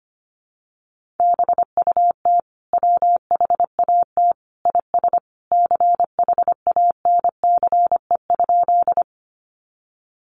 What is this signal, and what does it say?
Morse code, heard as a keyed tone, about 25 wpm, 700 Hz. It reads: BUT WHAT IS CHANCE?
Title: Sentences from Top 800 Words 25wpm